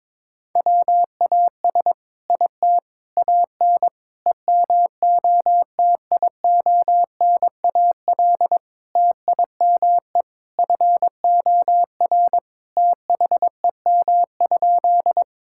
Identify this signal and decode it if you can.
Morse code, heard as a keyed tone, about 22 wpm, 700 Hz. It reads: WAS IT AN EMOTIONAL TIME FOR THEM?
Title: Sentences from Top 1000 Words - Review Part 5 - 22wpm